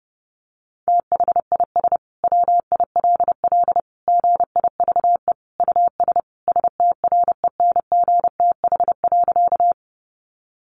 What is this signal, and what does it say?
Morse code, heard as a keyed tone, about 30 wpm, 700 Hz. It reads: THIS WILL GIVE US STRENGTH.
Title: Sentences from Top 1300 Words 30wpm